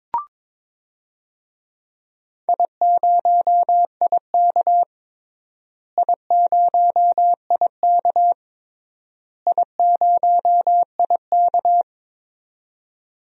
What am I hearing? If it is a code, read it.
Morse code, heard as a keyed tone, about 22 wpm, 700 Hz. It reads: I0IK I0IK I0IK
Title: DXCC-100 Call Signs - 4 Characters - Repeated 3x - 22wpm